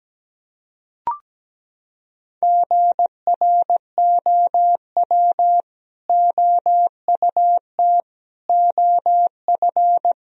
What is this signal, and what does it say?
Morse code, heard as a keyed tone, about 17 wpm, 700 Hz. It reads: GROW OUT OF
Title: Sentences from Top 1000 Words - Review Part 1 - 17wpm